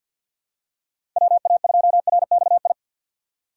Morse code, heard as a keyed tone, about 52 wpm, 700 Hz. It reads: WA2RXI
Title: USA Call Signs - 6 Characters - 6x Character Spacing - 50wpm